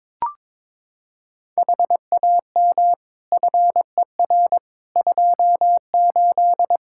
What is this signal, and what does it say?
Morse code, heard as a keyed tone, about 21 wpm, 700 Hz. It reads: HAM FER 28
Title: Practice QSO - Exchange 3 - More Info - 22wpm